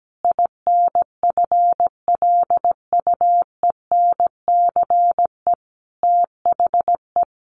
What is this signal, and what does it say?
Morse code, heard as a keyed tone, about 17 wpm, 700 Hz. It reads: INFLUENCE THE
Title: Sentences from Top 1100 Words 17wpm